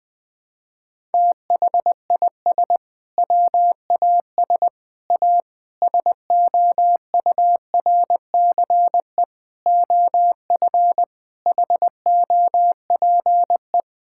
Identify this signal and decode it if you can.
Morse code, heard as a keyed tone, about 21 wpm, 700 Hz. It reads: THIS WAS A SOURCE OF HOPE
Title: Sentences from Top 1000 Words - Review Part 5 - 20wpm